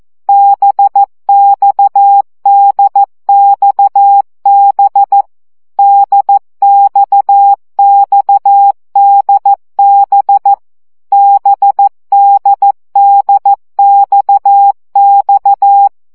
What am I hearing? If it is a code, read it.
Morse code, heard as a keyed tone, about 14 wpm, 800 Hz. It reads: BXDXB DXXDB BDDXX
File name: CW gyakorlas d b x 12 baud